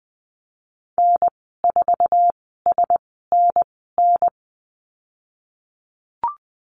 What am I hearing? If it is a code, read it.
Morse code, heard as a keyed tone, about 20 wpm, spaced as for 15 wpm, 700 Hz. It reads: N4SNN
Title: 076 Lesson 13g - Letter L - Call Signs 20wpm